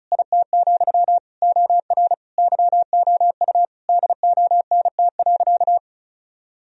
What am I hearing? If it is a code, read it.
Morse code, heard as a keyed tone, about 34 wpm, 700 Hz. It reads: IT, OR YOU DONT.
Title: Sentences from Top 1000 Words - Review Part 4 - 35wpm